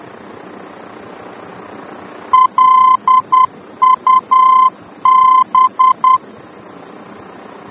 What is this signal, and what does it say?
Morse code, heard as a keyed tone, about 10 wpm, 1030 Hz. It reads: LUB
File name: germany / mx / LUB-D-VOR